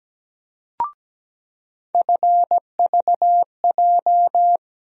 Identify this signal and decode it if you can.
Morse code, heard as a keyed tone, about 17 wpm, 700 Hz. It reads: FVJ
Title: USA Call Sign Postfixes - 3 Letters 17wpm